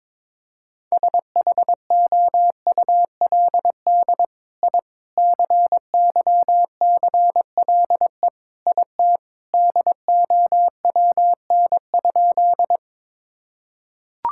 Morse code, heard as a keyed tone, about 22 wpm, 700 Hz. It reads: SHOULD I CYCLE IT DOWN?
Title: Sentences from Top 1000 Words - Review Part 6 - 22wpm